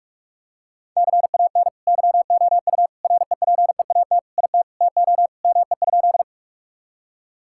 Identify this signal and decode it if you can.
Morse code, heard as a keyed tone, about 46 wpm, 700 Hz. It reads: CAN YOU REPEAT IT TO ME?
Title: Sentences from Top 1600 Words 45wpm